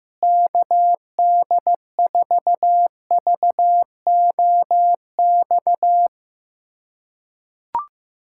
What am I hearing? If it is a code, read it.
Morse code, heard as a keyed tone, about 15 wpm, 700 Hz. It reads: KD4VOX
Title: USA Callsign DE Callsign - 6 Characters - 15wpm